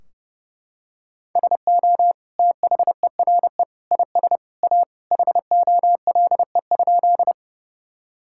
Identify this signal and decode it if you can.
Morse code, heard as a keyed tone, about 30 wpm, 700 Hz. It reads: SO THERE IS A HOLE?
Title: Sentences from Top 1600 Words 30wpm